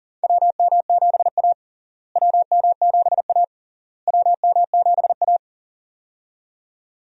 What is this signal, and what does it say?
Morse code, heard as a keyed tone, about 40 wpm, 700 Hz. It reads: WM7A WM7A WM7A
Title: USA Call Signs - 4 Characters - Repeated 3x - 40wpm